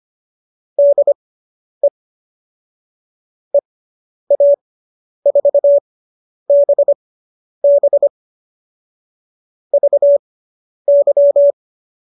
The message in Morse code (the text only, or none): DE EA4BB VY